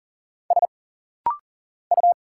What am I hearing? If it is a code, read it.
Morse code, heard as a keyed tone, about 40 wpm, 700 Hz. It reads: S U
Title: Single Letters - Rapid-Fire 40wpm